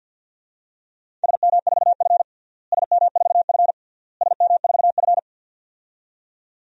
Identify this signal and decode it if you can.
Morse code, heard as a keyed tone, about 50 wpm, 700 Hz. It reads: SM4F SM4F SM4F
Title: DXCC-100 Call Signs - 4 Characters - Repeated 3x - 50wpm